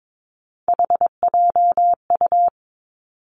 Morse code, heard as a keyed tone, about 22 wpm, 700 Hz. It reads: HJU
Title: Three Challenging Letter-Number Combinations 22wpm